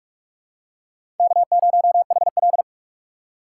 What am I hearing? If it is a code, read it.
Morse code, heard as a keyed tone, about 46 wpm, 700 Hz. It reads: K0HL
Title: USA Call Signs - 4 Characters - 2x Character Spacing - 45wpm